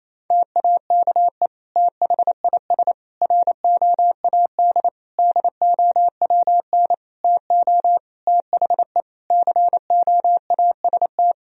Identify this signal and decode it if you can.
Morse code, heard as a keyed tone, about 28 wpm, 700 Hz. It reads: TAKE THIS ROAD DOWN TO THE COAST
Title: Sentences from Top 1000 Words - Review Part 4 - 28wpm